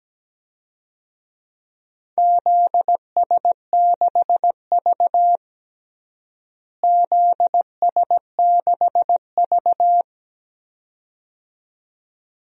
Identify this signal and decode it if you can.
Morse code, heard as a keyed tone, about 17 wpm, 700 Hz. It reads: ZS6V ZS6V
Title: DXCC-100 Call Signs - 4 Characters - Repeated 2x - 17wpm